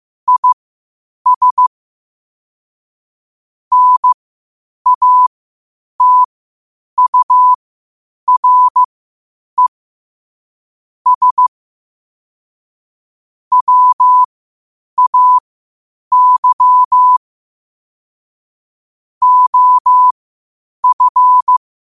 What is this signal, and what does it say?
Morse code, heard as a keyed tone, about 15 wpm, spaced as for 8 wpm, 1000 Hz. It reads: IS NATURE S WAY OF
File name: qotdfiles / 8wpm